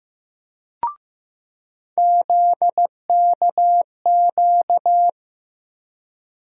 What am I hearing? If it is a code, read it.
Morse code, heard as a keyed tone, about 16 wpm, 700 Hz. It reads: ZKQ
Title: USA Call Sign Postfixes - 3 Letters 15wpm